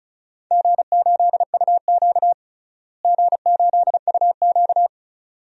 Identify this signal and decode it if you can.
Morse code, heard as a keyed tone, about 36 wpm, 700 Hz. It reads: G8UQ G8UQ
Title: DXCC-100 Call Signs - 4 Characters - Repeated 2x - 35wpm